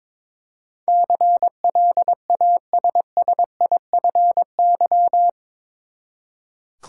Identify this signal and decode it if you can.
Morse code, heard as a keyed tone, about 22 wpm, 700 Hz. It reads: CLASSIFY
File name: practice / 22wpm / Eight Letter Words - 22wpm